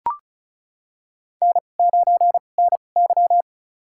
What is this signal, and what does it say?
Morse code, heard as a keyed tone, about 36 wpm, spaced as for 25 wpm, 700 Hz. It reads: N9NY
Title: USA Call Signs - 2x Character Spacing - 35wpm